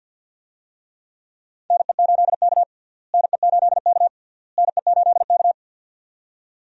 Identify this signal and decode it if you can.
Morse code, heard as a keyed tone, about 49 wpm, 700 Hz. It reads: NE8K NE8K NE8K
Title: USA Call Signs - 4 Characters - Repeated 3x - 50wpm